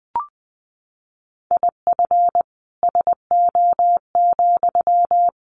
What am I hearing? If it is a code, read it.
Morse code, heard as a keyed tone, about 20 wpm, 700 Hz. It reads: IF SO,
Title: Sentences from Top 1200 Words 20wpm